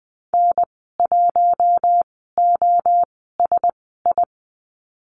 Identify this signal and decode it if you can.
Morse code, heard as a keyed tone, about 20 wpm, spaced as for 15 wpm, 700 Hz. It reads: N1OSI
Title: 055 Lesson 10g - Letter r - Call Signs 20wpm